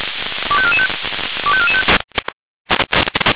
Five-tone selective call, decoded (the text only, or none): 25445 25445